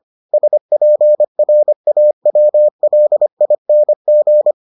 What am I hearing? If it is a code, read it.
Morse code, heard as a keyed tone, about 25 wpm, 600 Hz. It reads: SPRAWLING